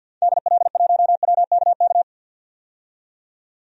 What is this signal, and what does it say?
Morse code, heard as a keyed tone, about 52 wpm, 700 Hz. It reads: DL1WKK
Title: DXCC-100 Call Signs - 6 Characters - 50wpm